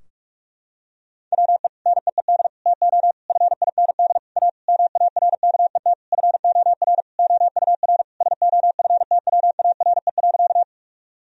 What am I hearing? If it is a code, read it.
Morse code, heard as a keyed tone, about 41 wpm, 700 Hz. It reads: WE NEED TO FIND A MARKET FOR OUR SOFTWARE.
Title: Sentences from Top 1000 Words - Review Part 3 - 45wpm